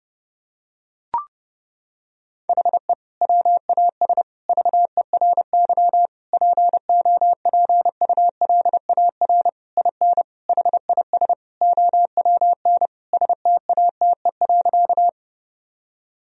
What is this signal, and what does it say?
Morse code, heard as a keyed tone, about 30 wpm, 700 Hz. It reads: HE WAS VERY POPULAR IN HIS OWN STATE.
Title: Sentences from Top 1000 Words - Review Part 3 - 30wpm